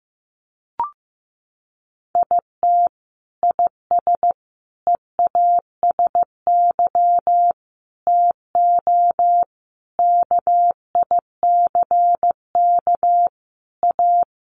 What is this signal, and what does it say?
Morse code, heard as a keyed tone, about 15 wpm, 700 Hz. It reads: IT IS EASY TO KICK A
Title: Sentences from Top 1300 Words 15wpm